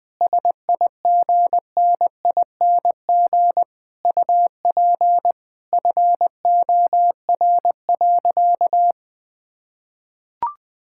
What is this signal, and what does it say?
Morse code, heard as a keyed tone, about 20 wpm, 700 Hz. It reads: SIGNING UP FOR.
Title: Sentences with 7-Word Phrases - 20wpm